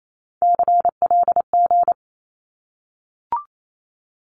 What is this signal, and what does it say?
Morse code, heard as a keyed tone, about 28 wpm, 700 Hz. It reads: CLG
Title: Abbreviations 28wpm